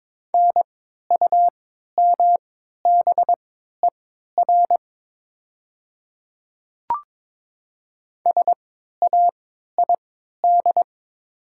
Morse code, heard as a keyed tone, about 22 wpm, spaced as for 12 wpm, 700 Hz. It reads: NUMBER SAID
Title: 3x Character Spacing - Top 100 Words - 22wpm